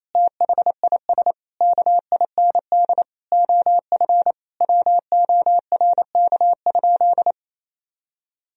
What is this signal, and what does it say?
Morse code, heard as a keyed tone, about 27 wpm, 700 Hz. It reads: THIS KIND OF WORK?
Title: Sentences from Top 1000 Words - Review Part 3 - 28wpm